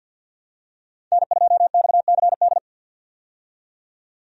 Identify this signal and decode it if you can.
Morse code, heard as a keyed tone, about 52 wpm, 700 Hz. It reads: N2XCD
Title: USA Call Signs - 5 Characters - 50wpm